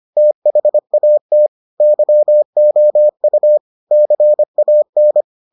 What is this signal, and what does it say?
Morse code, heard as a keyed tone, about 25 wpm, 600 Hz. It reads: THAT YOU CAN